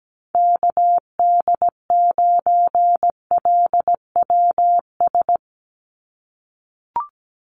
Morse code, heard as a keyed tone, about 17 wpm, 700 Hz. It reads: KD9LWS
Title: USA Call Signs 17wpm